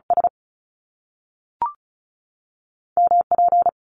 Morse code, heard as a keyed tone, about 36 wpm, 700 Hz. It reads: S M P
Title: Top Two Letter Combinations in Words 35wpm